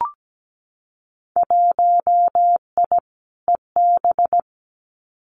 Morse code, heard as a keyed tone, about 17 wpm, 700 Hz. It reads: 1I EB
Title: Winter Field Day Exchanges Encore 17wpm